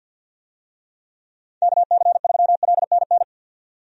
Morse code, heard as a keyed tone, about 47 wpm, 700 Hz. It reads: KC3LNN
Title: USA Call Signs - 6 Characters - 50wpm